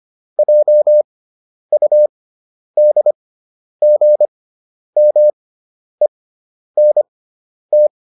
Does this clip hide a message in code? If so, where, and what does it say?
Yes: Morse code from 0.4 to 7.9 s, heard as a keyed tone, about 25 wpm, spaced as for 10 wpm, 600 Hz. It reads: JUDGMENT